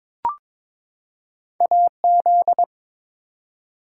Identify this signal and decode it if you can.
Morse code, heard as a keyed tone, about 23 wpm, 700 Hz. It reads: AZ
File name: ARRL Sections v2 - 22wpm